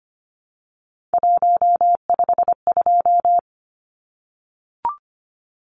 Morse code, heard as a keyed tone, about 25 wpm, 700 Hz. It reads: 152
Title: Numbers - 3-Digit 25wpm